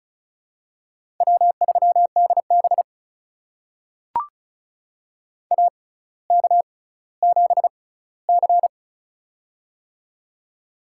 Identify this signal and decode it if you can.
Morse code, heard as a keyed tone, about 34 wpm, spaced as for 18 wpm, 700 Hz. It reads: W3DB AK7C
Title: USA Call Signs - 4 Characters - 6x Character Spacing - 35wpm